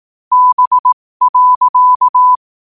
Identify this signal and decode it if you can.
Morse code, heard as a keyed tone, about 18 wpm, spaced as for 15 wpm, 1000 Hz. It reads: B.
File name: qotdfiles / 15wpm